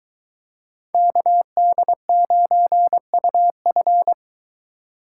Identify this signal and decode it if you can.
Morse code, heard as a keyed tone, about 23 wpm, 700 Hz. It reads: KD9UF